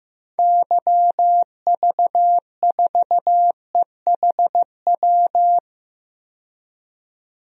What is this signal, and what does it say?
Morse code, heard as a keyed tone, about 15 wpm, 700 Hz. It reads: YV4EHW
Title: DXCC-100 Call Signs - 6 Characters - 15wpm